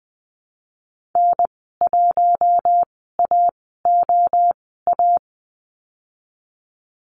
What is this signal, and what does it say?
Morse code, heard as a keyed tone, about 20 wpm, spaced as for 15 wpm, 700 Hz. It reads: N1AOA
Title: 048 Lesson 9g - Number 4 - Call Signs 20wpm